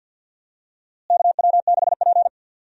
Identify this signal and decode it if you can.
Morse code, heard as a keyed tone, about 45 wpm, 700 Hz. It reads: KW6P
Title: USA Call Signs - 4 Characters - Repeated 2x - 50wpm